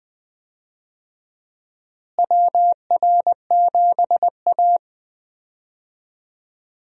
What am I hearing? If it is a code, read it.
Morse code, heard as a keyed tone, about 20 wpm, 700 Hz. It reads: WR7A
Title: USA Call Signs - 4 Characters - Repeated 2x - 20wpm